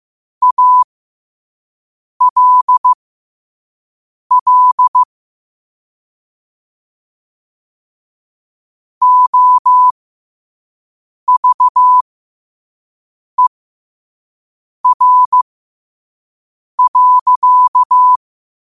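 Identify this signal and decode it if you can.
Morse code, heard as a keyed tone, about 15 wpm, spaced as for 5 wpm, 1000 Hz. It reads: ALL OVER.
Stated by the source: Morse at 5 wpm.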